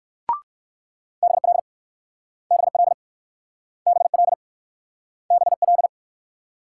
Morse code, heard as a keyed tone, about 62 wpm, 700 Hz. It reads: BL BL BL BL